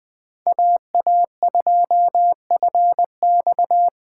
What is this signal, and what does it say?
Morse code, heard as a keyed tone, about 20 wpm, 700 Hz. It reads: AA2FX
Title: USA Call Signs - 5 Characters - 4x Character Spacing - 20wpm